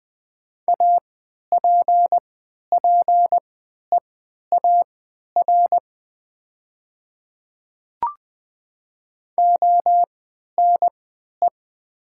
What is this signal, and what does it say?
Morse code, heard as a keyed tone, about 20 wpm, spaced as for 11 wpm, 700 Hz. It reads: APPEAR ONE
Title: 3x Character Spacing - Top 500 Words - 20wpm